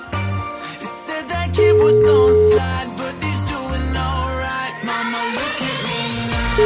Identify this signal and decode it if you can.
Ringback tone (Europe, 425 Hz), with noise.